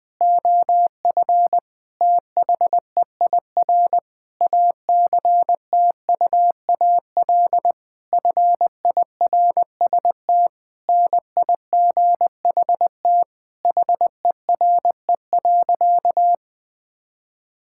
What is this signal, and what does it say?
Morse code, heard as a keyed tone, about 21 wpm, 700 Hz. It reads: OF THEIR ACTUAL FIRST NIGHT HERE.
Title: Sentences from Top 1000 Words - Review Part 5 - 20wpm